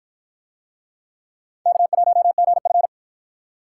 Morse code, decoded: K1GF

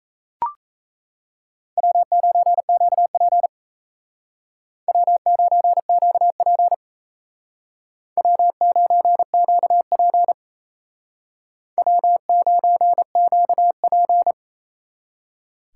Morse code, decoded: W9QP W9QP W9QP W9QP

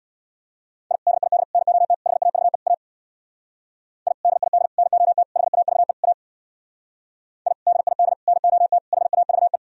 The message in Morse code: I DID NOT HAVE A I DID NOT HAVE A I DID NOT HAVE

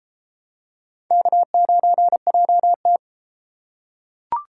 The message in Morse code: K9JT

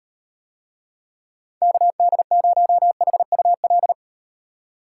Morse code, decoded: KD0HUL

